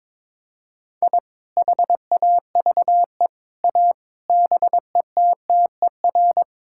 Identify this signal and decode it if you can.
Morse code, heard as a keyed tone, about 22 wpm, 700 Hz. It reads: I HAVE A BETTER
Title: Sentences from Top 1000 Words - Review Part 5 - 22wpm